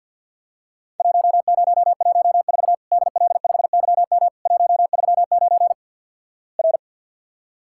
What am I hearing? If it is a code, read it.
Morse code, heard as a keyed tone, about 48 wpm, 700 Hz. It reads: 1014 DL5YM 139 R
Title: Worked All Europe DX Contest - QTC Exchange - 50wpm